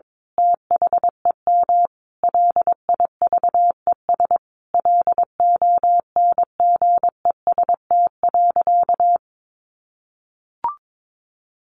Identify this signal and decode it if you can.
Morse code, heard as a keyed tone, about 22 wpm, 700 Hz. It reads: THEM LIVES LONGEST.
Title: A Sound Of Thunder 22wpm